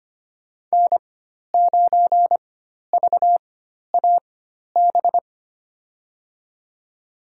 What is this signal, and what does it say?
Morse code, heard as a keyed tone, about 25 wpm, spaced as for 12 wpm, 700 Hz. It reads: N9VAB
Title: USA Call Signs - 4x Character Spacing - 25wpm